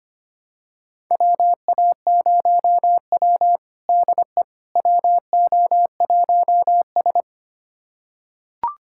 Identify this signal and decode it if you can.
Morse code, heard as a keyed tone, about 25 wpm, 700 Hz. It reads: WA0W DE WO1S
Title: USA Callsign DE Callsign - 4 Characters - 25wpm